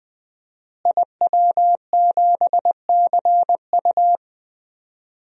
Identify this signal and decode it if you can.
Morse code, heard as a keyed tone, about 20 wpm, 700 Hz. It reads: IW7CU
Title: DXCC-100 Call Signs - 5 Characters - Repeated 2x - 20wpm